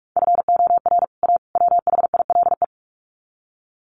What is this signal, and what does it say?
Morse code, heard as a keyed tone, about 44 wpm, 700 Hz. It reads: FOR A WHILE
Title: Sets of 3 Words 45wpm